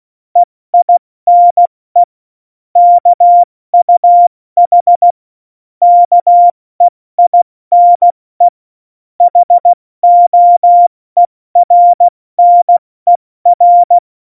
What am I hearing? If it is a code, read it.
Morse code, heard as a keyed tone, about 16 wpm, 700 Hz. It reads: EINE KUH KEINE HOERNER
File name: Morsekurs / Witze / Witz 14 Juli 2020, 70 F80